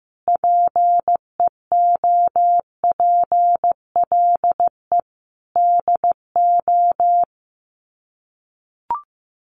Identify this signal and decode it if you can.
Morse code, heard as a keyed tone, about 15 wpm, 700 Hz. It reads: PEOPLE DO
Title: Sets of 5 Words 15wpm